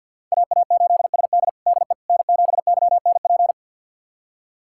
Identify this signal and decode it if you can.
Morse code, heard as a keyed tone, about 52 wpm, 700 Hz. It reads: AA8SD DE N7YNP
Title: USA Callsign DE Callsign - 5 Characters - 50wpm